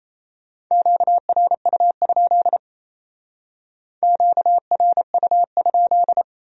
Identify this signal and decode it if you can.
Morse code, heard as a keyed tone, about 35 wpm, 700 Hz. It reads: QRU? QRU?
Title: Speed-Racing Q-Codes - 28wpm